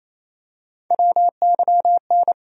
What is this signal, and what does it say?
Morse code, heard as a keyed tone, about 28 wpm, 700 Hz. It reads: WYN
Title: USA Call Sign Postfixes - 3 Letters 28wpm